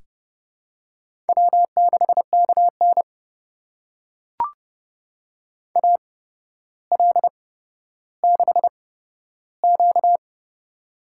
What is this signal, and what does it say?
Morse code, heard as a keyed tone, about 30 wpm, spaced as for 13 wpm, 700 Hz. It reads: W6KN AL6Q
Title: USA Call Signs - 4 Characters - 8x Character Spacing - 30wpm